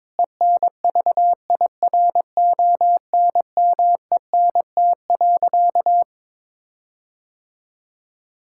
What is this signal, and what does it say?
Morse code, heard as a keyed tone, about 22 wpm, 700 Hz. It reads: ENVIRONMENT.
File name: Difficult Sentences with Hints - 22wpm